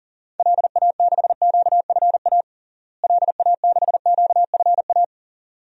Morse code, heard as a keyed tone, about 40 wpm, 700 Hz. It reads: LA6QFA LA6QFA
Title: DXCC-100 Call Signs - 6 Characters - Repeated 2x - 40wpm